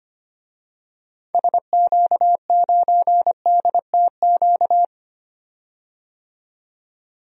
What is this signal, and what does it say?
Morse code, heard as a keyed tone, about 25 wpm, 700 Hz. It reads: SQ9DTQ
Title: DXCC-100 Call Signs - 6 Characters - 25wpm